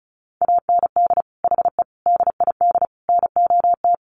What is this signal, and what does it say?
Morse code, heard as a keyed tone, about 34 wpm, 700 Hz. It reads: AND HE DID NOT
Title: Sentences from Top 1300 Words 35wpm